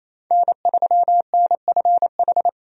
Morse code, heard as a keyed tone, about 28 wpm, 700 Hz. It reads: N3NFH
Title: USA Call Signs - 5 Characters - 28wpm